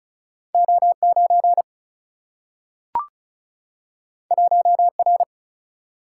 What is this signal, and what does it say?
Morse code, heard as a keyed tone, about 35 wpm, 700 Hz. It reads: O9 1R
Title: Single-Letter Single-Number 35wpm